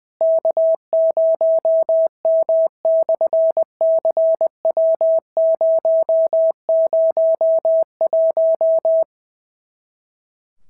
Morse code, heard as a keyed tone, about 20 wpm, 650 Hz. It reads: K0M/CW001
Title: SOTA QSO Practice 20wpm